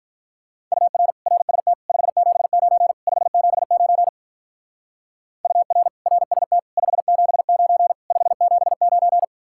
Morse code, heard as a keyed tone, about 45 wpm, 700 Hz. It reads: UR RST 579 579 UR RST 579 579